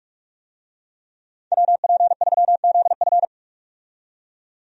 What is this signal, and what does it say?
Morse code, heard as a keyed tone, about 44 wpm, 700 Hz. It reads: WP3ZF
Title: USA Call Signs - 5 Characters - 45wpm